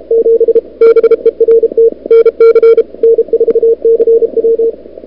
Morse code, heard as a keyed tone, about 37 wpm, 440 Hz. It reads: 7 BERT NC N4CW
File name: n6tr / Fab2007CWSprintHour1